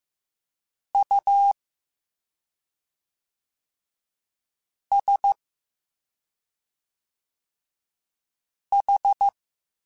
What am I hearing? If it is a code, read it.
Morse code, heard as a keyed tone, about 15 wpm, 800 Hz. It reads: U S H